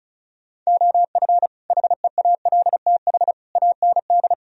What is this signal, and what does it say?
Morse code, heard as a keyed tone, about 46 wpm, spaced as for 41 wpm, 700 Hz. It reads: OF HEALTH AND
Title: Speed-Racing - Sets of 5 Words 35wpm